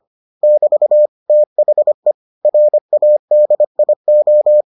Morse code, heard as a keyed tone, about 25 wpm, 600 Hz. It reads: <BT> THE RADIO